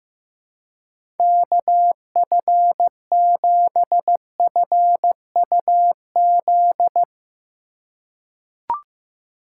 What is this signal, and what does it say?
Morse code, heard as a keyed tone, about 15 wpm, 700 Hz. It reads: KF7FUZ
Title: USA Call Signs - 6 Characters - 4x Character Spacing - 15wpm